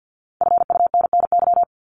Morse code, heard as a keyed tone, about 50 wpm, 700 Hz. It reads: FUNNY
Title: Top 1751-2000 Words 50wpm